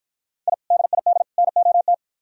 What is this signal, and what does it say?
Morse code, heard as a keyed tone, about 50 wpm, 700 Hz. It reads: I DID NOT